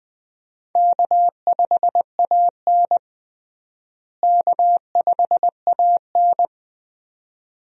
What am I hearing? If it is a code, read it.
Morse code, heard as a keyed tone, about 20 wpm, 700 Hz. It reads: K5AN K5AN
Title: USA Call Signs - 4 Characters - Repeated 2x - 20wpm